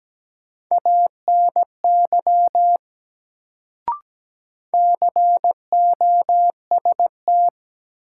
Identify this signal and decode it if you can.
Morse code, heard as a keyed tone, about 17 wpm, 700 Hz. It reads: ANY COST
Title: Sets of 3 Words - New Word Drill - 17wpm